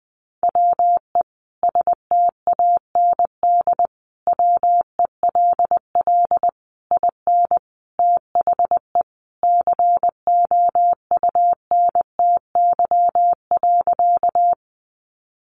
Morse code, heard as a keyed tone, about 20 wpm, 700 Hz. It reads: WE STAND WELL IN THE COUNTY.
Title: Sentences from Top 1200 Words 20wpm